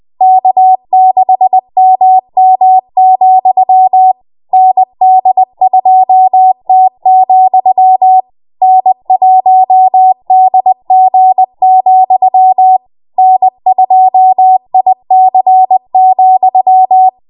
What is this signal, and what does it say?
Morse code, heard as a keyed tone, about 20 wpm, 750 Hz. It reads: K6MM, ND2T, N1DG, N2IC,
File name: Morse / 20WPM